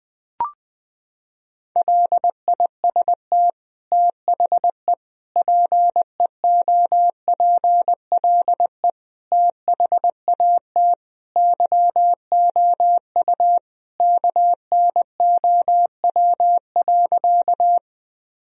Morse code, heard as a keyed tone, about 20 wpm, 700 Hz. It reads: LIST THE PEOPLE THAT YOU KNOW.
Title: Sentences from Top 1000 Words - Review Part 2 - 20wpm